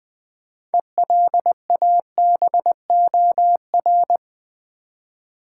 Morse code, heard as a keyed tone, about 20 wpm, 700 Hz. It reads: ELABOR
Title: Build A Word - 3 Letters At A Time - 9-Letter Words - 20wpm